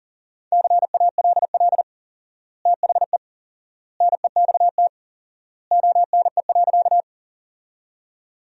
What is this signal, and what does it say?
Morse code, heard as a keyed tone, about 40 wpm, 700 Hz. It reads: CALL THE NEXT ONE.
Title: 4x Word-Spacing - Sentences from Top 300 Words - 40wpm